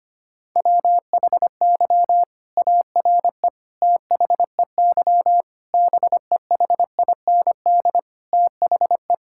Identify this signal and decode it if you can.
Morse code, heard as a keyed tone, about 25 wpm, 700 Hz. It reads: WHY ARE THEY BEHIND THE
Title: Sentences from Top 1000 Words - Review Part 6 - 25wpm